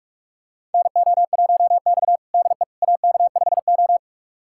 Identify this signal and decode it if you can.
Morse code, heard as a keyed tone, about 41 wpm, 700 Hz. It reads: NO1X DE AK5O
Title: USA Callsign DE Callsign - 4 Characters - 45wpm